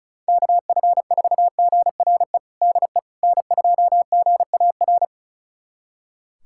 Morse code, heard as a keyed tone, about 34 wpm, 700 Hz. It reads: KF4GRE DE N2GAR
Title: USA Callsign DE Callsign 35wpm